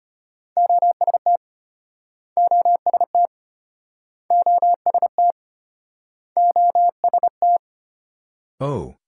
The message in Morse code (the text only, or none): OST OST OST OST